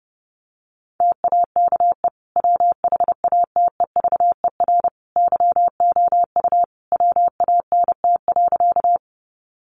Morse code, heard as a keyed tone, about 30 wpm, 700 Hz. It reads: TAKE WHATEVER YOU WANT.
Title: Sentences from Top 1100 Words 30wpm